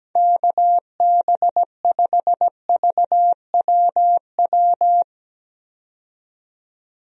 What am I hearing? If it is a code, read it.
Morse code, heard as a keyed tone, about 17 wpm, 700 Hz. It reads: KB5VWW
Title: USA Call Signs - 6 Characters - 17wpm